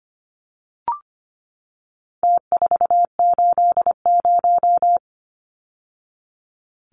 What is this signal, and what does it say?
Morse code, heard as a keyed tone, about 25 wpm, 700 Hz. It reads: T480
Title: Numbers - Contest Style - 4-Digit 25wpm